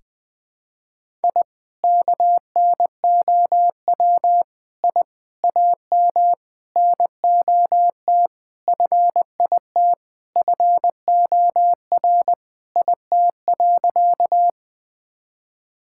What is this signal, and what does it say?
Morse code, heard as a keyed tone, about 21 wpm, 700 Hz. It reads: I KNOW I AM NOT FIT FOR IT.
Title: Sentences from Top 1000 Words - Review Part 5 - 20wpm